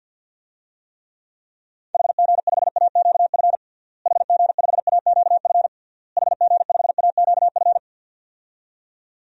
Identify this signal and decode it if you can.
Morse code, heard as a keyed tone, about 52 wpm, 700 Hz. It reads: HG5AQF HG5AQF HG5AQF
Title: DXCC-100 Call Signs - 6 Characters - Repeated 3x - 50wpm